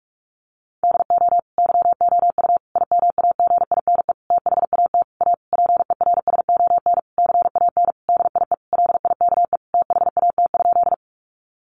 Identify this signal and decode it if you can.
Morse code, heard as a keyed tone, about 44 wpm, 700 Hz. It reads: DO YOU IMAGINE THAT A PERSON CAN DIE LIKE THAT?
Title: Sentences from Top 1400 Words 45wpm